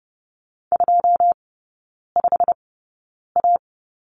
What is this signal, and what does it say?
Morse code, heard as a keyed tone, about 30 wpm, spaced as for 9 wpm, 700 Hz. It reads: 25A
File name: Lesson 19e - Letter w - Three Characters 30wpm